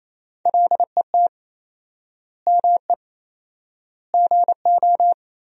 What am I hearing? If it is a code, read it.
Morse code, heard as a keyed tone, about 28 wpm, 700 Hz. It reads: LET ME GO
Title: 4x Word-Spacing - Sentences from Top 300 Words - 28wpm